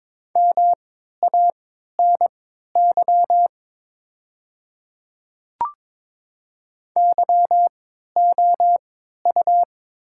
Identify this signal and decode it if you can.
Morse code, heard as a keyed tone, about 22 wpm, spaced as for 12 wpm, 700 Hz. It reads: MANY YOU
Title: 3x Character Spacing - Top 100 Words - 22wpm